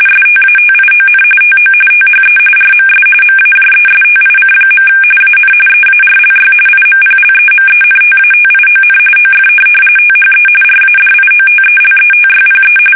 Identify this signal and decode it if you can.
RTTY, about 45.45 baud. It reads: STATION TO THE RMS MAJESTIC. AN EARLY IMPLEMENTATION OF THE RADIOTELETYPE
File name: fsk850 / teletype-RTTY